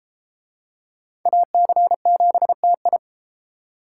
Morse code, heard as a keyed tone, about 28 wpm, 700 Hz. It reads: AC7TI